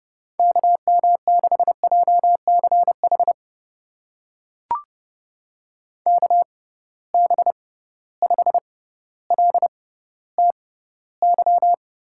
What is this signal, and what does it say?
Morse code, heard as a keyed tone, about 30 wpm, spaced as for 15 wpm, 700 Hz. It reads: KM6JCH KB5LTY
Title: USA Call Signs - 6 Characters - 6x Character Spacing - 30wpm